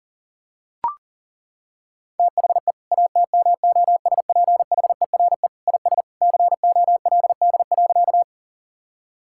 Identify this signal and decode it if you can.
Morse code, heard as a keyed tone, about 37 wpm, 700 Hz. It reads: THE ATMOSPHERE IS COLD.
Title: Sentences from Top 1600 Words 40wpm